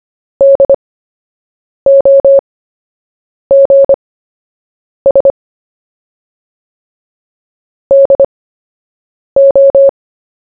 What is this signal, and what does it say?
Morse code, heard as a keyed tone, about 25 wpm, spaced as for 7 wpm, 550 Hz. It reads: DOGS DO
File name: cw / fists / Short-Story-1-7wpm